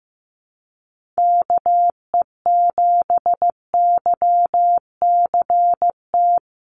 Morse code, heard as a keyed tone, about 15 wpm, 700 Hz. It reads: KE7YCT
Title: USA Call Signs Repeated 2x 15wpm